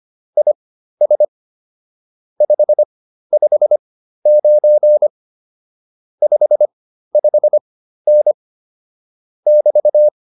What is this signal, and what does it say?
Morse code, heard as a keyed tone, about 25 wpm, spaced as for 13 wpm, 600 Hz. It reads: IS 559 55N <BT>